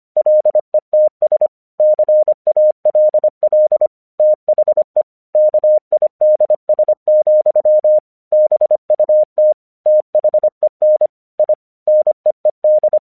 Morse code, heard as a keyed tone, about 25 wpm, 610 Hz. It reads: LETS CALL THE KIDS, BUT THEN I NEED
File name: 1 English 25wpm / AQC36